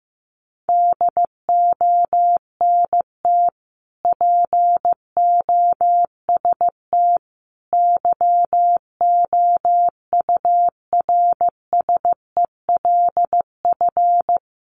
Morse code, heard as a keyed tone, about 15 wpm, 700 Hz. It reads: DONT POST YOURSELF
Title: Sentences from Top 1000 Words 15wpm